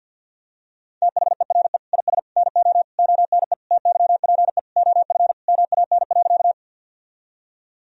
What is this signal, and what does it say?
Morse code, heard as a keyed tone, about 50 wpm, 700 Hz. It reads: THERE IS NO ONE TYPE OF MAN.
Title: Sentences from Top 1000 Words - Review Part 3 - 50wpm